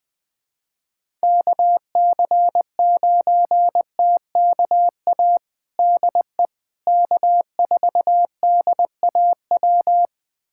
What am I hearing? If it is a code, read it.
Morse code, heard as a keyed tone, about 20 wpm, 700 Hz. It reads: KC9TKA DE K4DAW
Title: USA Callsign DE Callsign 20wpm